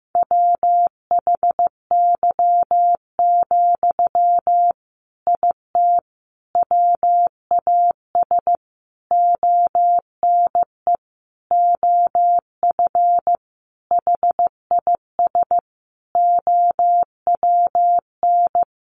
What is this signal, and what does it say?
Morse code, heard as a keyed tone, about 15 wpm, 700 Hz. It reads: WHY, IT WAS ONE OF HIS OWN
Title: Sentences from Top 300 Words 15wpm